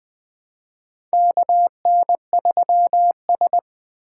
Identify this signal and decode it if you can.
Morse code, heard as a keyed tone, about 20 wpm, 700 Hz. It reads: KN3S
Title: USA Call Signs - 4 Characters - 4x Character Spacing - 20wpm